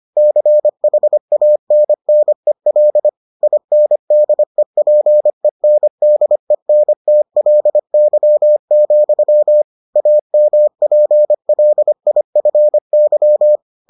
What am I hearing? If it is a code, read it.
Morse code, heard as a keyed tone, about 25 wpm, 600 Hz. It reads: CHANNEL INDEPENDENTLY, AMPLIFY